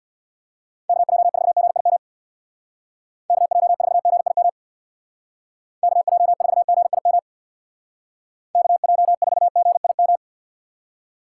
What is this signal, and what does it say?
Morse code, heard as a keyed tone, about 65 wpm, 700 Hz. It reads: KJ4GIM KJ4GIM KJ4GIM KJ4GIM